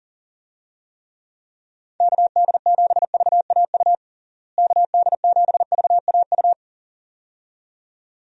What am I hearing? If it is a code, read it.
Morse code, heard as a keyed tone, about 40 wpm, 700 Hz. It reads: KD7VAU KD7VAU
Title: USA Call Signs - 6 Characters - Repeated 2x - 40wpm